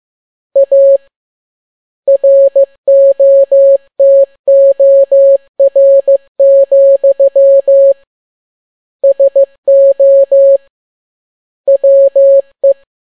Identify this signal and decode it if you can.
Morse code, heard as a keyed tone, about 15 wpm, 550 Hz. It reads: A ROTOR, SO WE
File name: n6tt p1 15C13